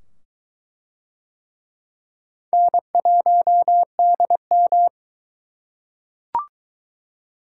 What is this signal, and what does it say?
Morse code, heard as a keyed tone, about 22 wpm, 700 Hz. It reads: N1DM